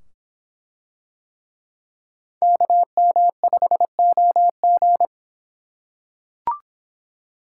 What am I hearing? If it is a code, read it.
Morse code, heard as a keyed tone, about 24 wpm, 700 Hz. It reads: KM5OG